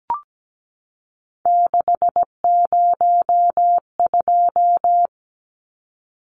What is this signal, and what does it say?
Morse code, heard as a keyed tone, about 17 wpm, 700 Hz. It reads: 602
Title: Numbers - 3-Digit 17wpm